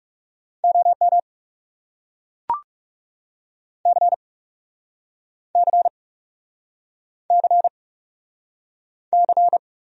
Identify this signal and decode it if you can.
Morse code, heard as a keyed tone, about 40 wpm, 700 Hz. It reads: OM C C C C